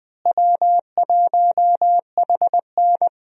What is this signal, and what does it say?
Morse code, heard as a keyed tone, about 20 wpm, 700 Hz. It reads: W1HN
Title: USA Call Signs - 4 Characters - Repeated 3x - 20wpm